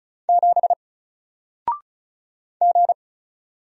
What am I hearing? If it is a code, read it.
Morse code, heard as a keyed tone, about 36 wpm, 700 Hz. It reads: 7 G
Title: Single Character (Letter:Number:Punctuation:Prosign) 35wpm